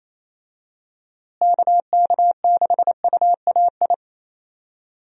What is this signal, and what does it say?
Morse code, heard as a keyed tone, about 28 wpm, 700 Hz. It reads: KK6UAI
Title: USA Call Signs - 6 Characters - Repeated 3x - 28wpm